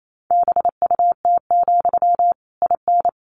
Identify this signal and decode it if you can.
Morse code, heard as a keyed tone, about 28 wpm, 700 Hz. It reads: BUT, IN
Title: Sentences from Top 1500 Words 28wpm